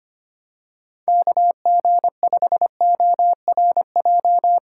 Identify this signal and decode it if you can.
Morse code, heard as a keyed tone, about 25 wpm, 700 Hz. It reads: KG5ORJ
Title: USA Call Signs - 6 Characters - 6x Character Spacing - 25wpm